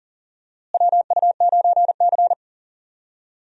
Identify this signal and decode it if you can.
Morse code, heard as a keyed tone, about 40 wpm, 700 Hz. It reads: WU9C
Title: USA Call Signs - 4 Characters - Repeated 2x - 40wpm